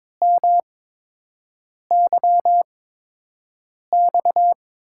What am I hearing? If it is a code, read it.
Morse code, heard as a keyed tone, about 22 wpm, spaced as for 6 wpm, 700 Hz. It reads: MYX
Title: USA Call Signs - 8x Character Spacing - 22wpm